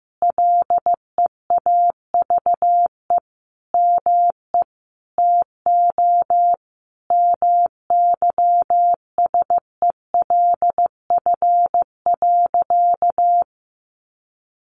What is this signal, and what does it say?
Morse code, heard as a keyed tone, about 15 wpm, 700 Hz. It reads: LEAVE ME TO MYSELF.
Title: Sentences from Top 1300 Words 15wpm